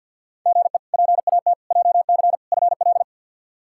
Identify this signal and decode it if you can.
Morse code, heard as a keyed tone, about 47 wpm, 690 Hz. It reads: GE PAT JC FL
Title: K1USN SST QSO Practice - 50wpm